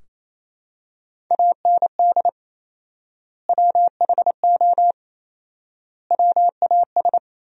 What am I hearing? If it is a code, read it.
Morse code, heard as a keyed tone, about 28 wpm, 700 Hz. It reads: AND WHO WAS
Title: 4x Word-Spacing - Sentences from Top 100 Words - 28wpm